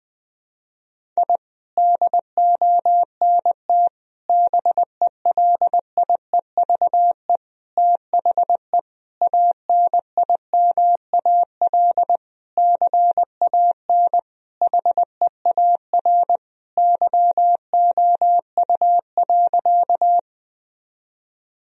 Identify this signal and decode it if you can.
Morse code, heard as a keyed tone, about 21 wpm, 700 Hz. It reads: I DONT BELIEVE THE ANIMAL CAN HEAR YOU.
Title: Sentences from Top 1000 Words - Review Part 1 - 20wpm